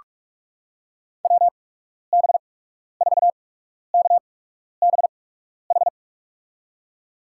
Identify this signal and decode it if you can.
Morse code, heard as a keyed tone, about 46 wpm, spaced as for 12 wpm, 700 Hz. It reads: WB4KBH
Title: USA Call Signs - 6 Characters - 8x Character Spacing - 45wpm